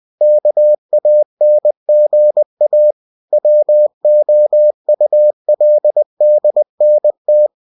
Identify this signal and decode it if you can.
Morse code, heard as a keyed tone, about 20 wpm, 600 Hz. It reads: KANGA WOULDNT
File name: cw-books / winnie-the-pooh / winnie-20wpm-page-0063